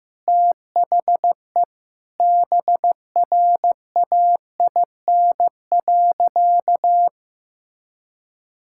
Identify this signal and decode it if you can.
Morse code, heard as a keyed tone, about 15 wpm, 700 Hz. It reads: THE BRAIN.
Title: Sentences from Top 1000 Words - Review Part 6 - 15wpm